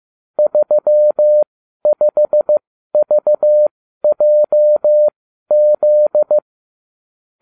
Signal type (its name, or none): Morse code